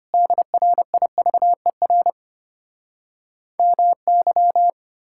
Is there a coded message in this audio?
Yes: Morse code from 0.1 to 4.7 s, heard as a keyed tone, about 33 wpm, 700 Hz. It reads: DRIVER MY